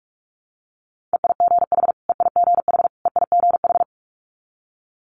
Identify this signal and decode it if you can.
Morse code, heard as a keyed tone, about 46 wpm, 700 Hz. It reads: EIGH EIGH EIGH
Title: Multi-letter Phonemes Repeated 3x 45wpm